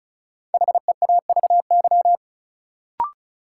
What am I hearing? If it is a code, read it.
Morse code, heard as a keyed tone, about 34 wpm, 700 Hz. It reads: HEAVY
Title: Sentences from Top 1100 Words - New Word Drill Alt - 35wpm